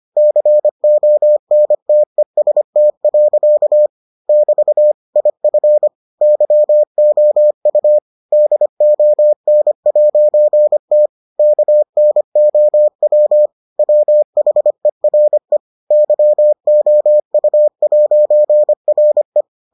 Morse code, heard as a keyed tone, about 25 wpm, 600 Hz. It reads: CONTEST. <BT> IF YOU DON'T KNOW WHERE YOU'RE